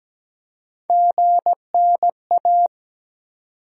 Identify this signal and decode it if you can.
Morse code, heard as a keyed tone, about 17 wpm, 700 Hz. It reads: GNA
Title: USA Call Sign Postfixes - 3 Letters 17wpm